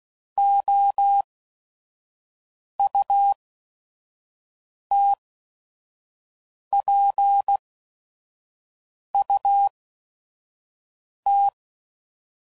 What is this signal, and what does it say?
Morse code, heard as a keyed tone, about 16 wpm, spaced as for 5 wpm, 800 Hz. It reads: OUTPUT
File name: CW Code Groups @5WPM